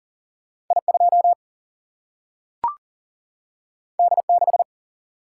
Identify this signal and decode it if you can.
Morse code, heard as a keyed tone, about 39 wpm, 700 Hz. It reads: I2 D6
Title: Single-Letter Single-Number 40wpm